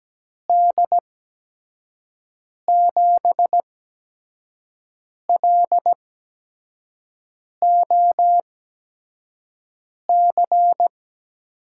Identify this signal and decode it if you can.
Morse code, heard as a keyed tone, about 17 wpm, spaced as for 5 wpm, 700 Hz. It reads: D7LOC
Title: USA Call Signs - 8x Character Spacing - 17wpm